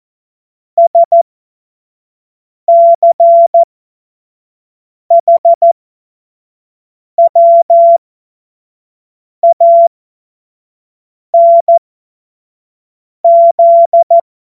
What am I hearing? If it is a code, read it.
Morse code, heard as a keyed tone, about 14 wpm, spaced as for 5 wpm, 680 Hz. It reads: SCHWANZ
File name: Morsekurs / Witze / Witz 05. April 2022 25 F70